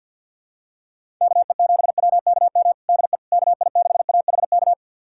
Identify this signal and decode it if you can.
Morse code, heard as a keyed tone, about 51 wpm, 700 Hz. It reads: KE7WKM DE KI6AHK
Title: USA Callsign DE Callsign - 6 Characters - 50wpm